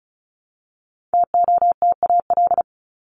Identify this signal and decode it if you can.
Morse code, heard as a keyed tone, about 34 wpm, 700 Hz. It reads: TOTAL
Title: LY Suffix - Top 100 Words with Base Words 35wpm